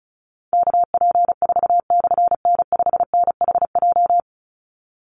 Morse code, heard as a keyed tone, about 34 wpm, 700 Hz. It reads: KP4/N5NHJ
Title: Call Signs Repeated 3x - 35wpm